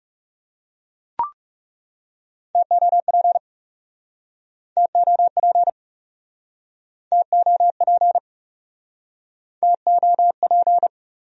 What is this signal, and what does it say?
Morse code, heard as a keyed tone, about 42 wpm, 700 Hz. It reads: TOP TOP TOP TOP